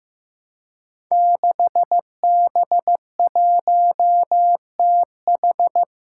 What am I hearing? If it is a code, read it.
Morse code, heard as a keyed tone, about 15 wpm, 700 Hz. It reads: 6B1TH
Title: Five Challenging Letter-Number Combinations 15wpm